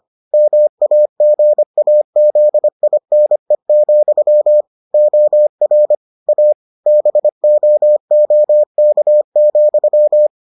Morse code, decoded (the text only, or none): MAGAZINE, OR A BOOK,